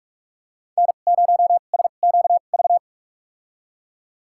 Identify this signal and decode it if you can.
Morse code, heard as a keyed tone, about 42 wpm, spaced as for 31 wpm, 700 Hz. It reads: N0SQV
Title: USA Call Signs - 2x Character Spacing - 45wpm